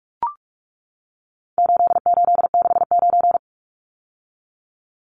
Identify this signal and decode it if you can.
Morse code, heard as a keyed tone, about 44 wpm, 700 Hz. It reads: 8869
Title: Numbers - 4-Digit 45wpm